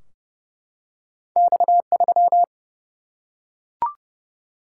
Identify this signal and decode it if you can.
Morse code, heard as a keyed tone, about 30 wpm, 700 Hz. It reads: X3
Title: Single-Letter Single-Number 30wpm